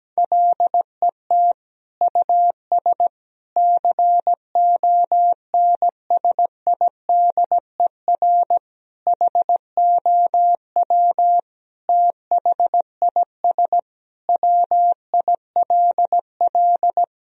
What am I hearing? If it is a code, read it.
Morse code, heard as a keyed tone, about 17 wpm, 700 Hz. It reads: LET US CONSIDER HOW THIS WILL
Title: Sentences from Top 1000 Words - Review Part 4 - 17wpm